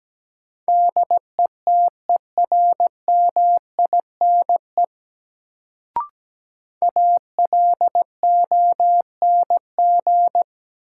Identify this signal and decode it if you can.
Morse code, heard as a keyed tone, about 17 wpm, 700 Hz. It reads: DETERMINE ALONG
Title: Sentences from Top 700 Words - New Word Drill - 17wpm